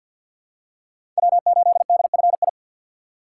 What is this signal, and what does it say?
Morse code, heard as a keyed tone, about 48 wpm, 700 Hz. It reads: W8DFI
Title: USA Call Signs - 3x Character Spacing - 50wpm